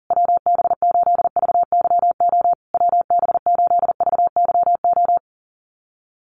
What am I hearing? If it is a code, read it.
Morse code, heard as a keyed tone, about 40 wpm, 700 Hz. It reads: WB8VYO WB8VYO
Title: USA Call Signs Repeated 3x 40wpm